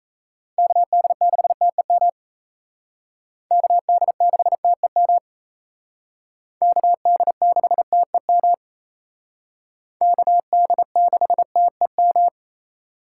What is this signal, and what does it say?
Morse code, heard as a keyed tone, about 35 wpm, 700 Hz. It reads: KD6TEM KD6TEM KD6TEM KD6TEM
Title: Speed Racing - USA 2x3 Call Signs 28wpm